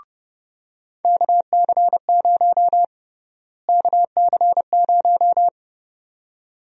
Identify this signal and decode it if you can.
Morse code, heard as a keyed tone, about 32 wpm, 700 Hz. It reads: KC0 KC0
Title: USA Call Sign Prefixes - Repeated 2x - 30wpm